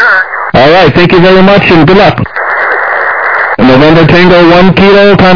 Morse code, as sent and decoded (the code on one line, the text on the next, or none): none